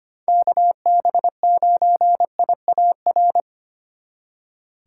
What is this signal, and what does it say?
Morse code, heard as a keyed tone, about 28 wpm, 700 Hz. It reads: KB9IAR
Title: Speed Racing - USA Call Signs 25wpm